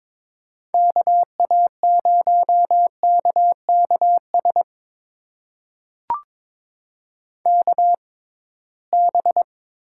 Morse code, heard as a keyed tone, about 22 wpm, spaced as for 17 wpm, 700 Hz. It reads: KA0KKS KB
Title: USA Call Signs - 6 Characters - 6x Character Spacing - 22wpm